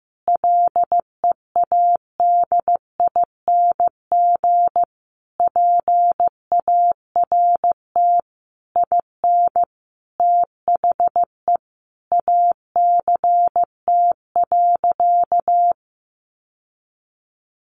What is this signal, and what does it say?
Morse code, heard as a keyed tone, about 15 wpm, 700 Hz. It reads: LEADING PART IN THE ACT.
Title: Sentences from Top 1100 Words 15wpm